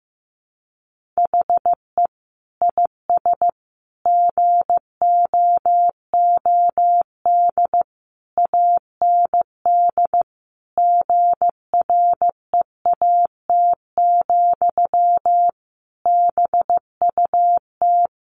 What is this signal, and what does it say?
Morse code, heard as a keyed tone, about 15 wpm, 700 Hz. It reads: HE IS GOOD AND GREAT, BUT
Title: Sentences from Top 1400 Words 15wpm